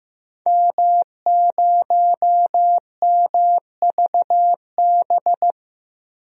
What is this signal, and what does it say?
Morse code, heard as a keyed tone, about 15 wpm, 700 Hz. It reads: M0MVB
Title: DXCC-100 Call Signs - 5 Characters - 15wpm